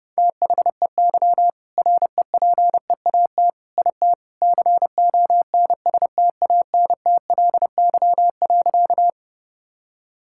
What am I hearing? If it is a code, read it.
Morse code, heard as a keyed tone, about 30 wpm, 700 Hz. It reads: THEY REPEAT IT CONSTANTLY.
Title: Sentences from Top 1600 Words 30wpm